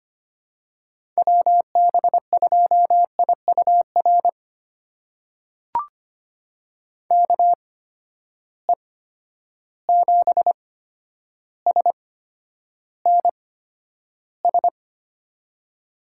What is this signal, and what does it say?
Morse code, heard as a keyed tone, about 25 wpm, spaced as for 11 wpm, 700 Hz. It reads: WB2IUR KE7SNS
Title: USA Call Signs - 6 Characters - 8x Character Spacing - 25wpm